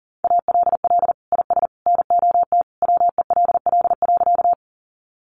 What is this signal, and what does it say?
Morse code, heard as a keyed tone, about 40 wpm, 700 Hz. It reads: ALL IS NOT WELL.
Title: Sentences from Top 600 Words 40wpm